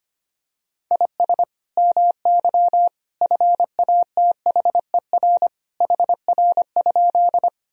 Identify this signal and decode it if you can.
Morse code, heard as a keyed tone, about 25 wpm, 700 Hz. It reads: IS MY FATHER HR?
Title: Sentences from Top 300 Words and Abbreviations 25wpm